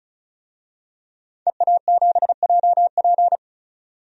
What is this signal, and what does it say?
Morse code, heard as a keyed tone, about 34 wpm, 700 Hz. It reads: EA7JP
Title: DXCC-100 Call Signs - 5 Characters - 35wpm